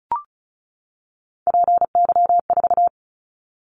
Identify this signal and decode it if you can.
Morse code, heard as a keyed tone, about 34 wpm, 700 Hz. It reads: PY4
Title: Callsign Prefixes for Top 100 DXCC - 35wpm